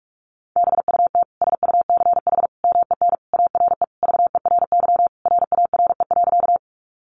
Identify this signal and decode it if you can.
Morse code, heard as a keyed tone, about 44 wpm, 700 Hz. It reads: BUT SUCH MEN ARE VERY RARE.
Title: Sentences from Top 1100 Words 45wpm